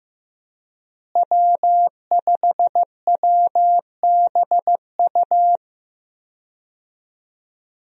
Morse code, heard as a keyed tone, about 15 wpm, 700 Hz. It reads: W5WBU
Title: USA Call Signs - 5 Characters - Repeated 2x - 15wpm